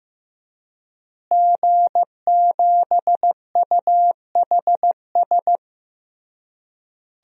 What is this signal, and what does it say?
Morse code, heard as a keyed tone, about 15 wpm, 700 Hz. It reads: G7UHS
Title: DXCC-100 Call Signs - 5 Characters - Repeated 3x - 15wpm